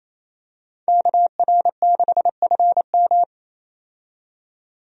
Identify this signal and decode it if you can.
Morse code, heard as a keyed tone, about 28 wpm, 700 Hz. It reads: KR6FM
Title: USA Call Signs - 5 Characters - 28wpm